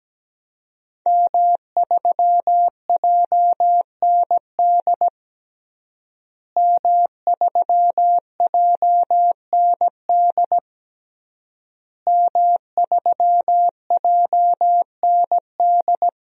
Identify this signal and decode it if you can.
Morse code, heard as a keyed tone, about 17 wpm, 700 Hz. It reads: M3JND M3JND M3JND
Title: DXCC-100 Call Signs - 5 Characters - Repeated 3x - 17wpm